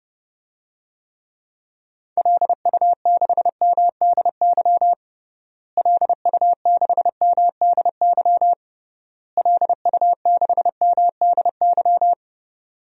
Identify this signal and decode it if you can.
Morse code, heard as a keyed tone, about 30 wpm, 700 Hz. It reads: LU6MDY LU6MDY LU6MDY
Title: DXCC-100 Call Signs - 6 Characters - Repeated 3x - 30wpm